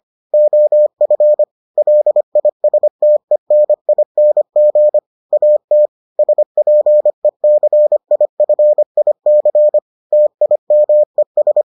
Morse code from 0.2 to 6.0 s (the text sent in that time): OF LISTENING AT